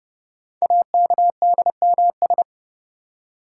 Morse code, heard as a keyed tone, about 29 wpm, 700 Hz. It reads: AKDMS
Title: Code Groups - 5 Letters - 30wpm